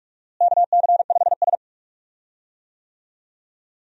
Morse code, heard as a keyed tone, about 41 wpm, 700 Hz. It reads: KC5S
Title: USA Call Signs - 4 Characters - Repeated 2x - 45wpm